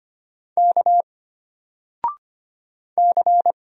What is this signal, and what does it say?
Morse code, heard as a keyed tone, about 25 wpm, 700 Hz. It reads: K C
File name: Single-Letter-Number-25wpm